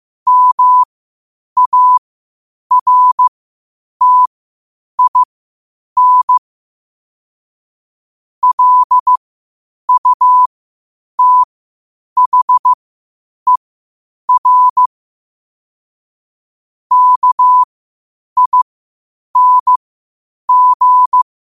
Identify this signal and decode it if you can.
Morse code, heard as a keyed tone, about 15 wpm, spaced as for 8 wpm, 1000 Hz. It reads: MARTIN LUTHER KING
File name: qotdfiles / 8wpm